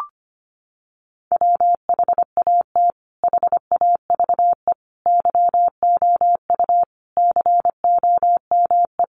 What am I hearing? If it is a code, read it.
Morse code, heard as a keyed tone, about 25 wpm, 700 Hz. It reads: WHAT HAVE YOU COME
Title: Sentences from Top 100 Words 25wpm